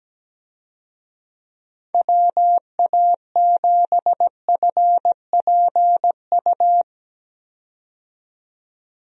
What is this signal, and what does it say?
Morse code, heard as a keyed tone, about 17 wpm, 700 Hz. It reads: WA7FPU
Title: USA Call Signs - 6 Characters - 17wpm